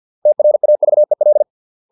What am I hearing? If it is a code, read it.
Morse code, heard as a keyed tone, about 50 wpm, 600 Hz. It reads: TRAVEL